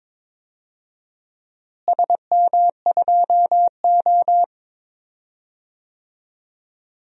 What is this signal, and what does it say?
Morse code, heard as a keyed tone, about 21 wpm, 700 Hz. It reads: SM2O
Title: DXCC-100 Call Signs - 4 Characters - Repeated 2x - 22wpm